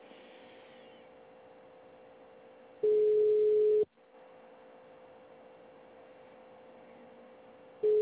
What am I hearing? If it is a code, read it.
Ringback tone (Europe, 425 Hz).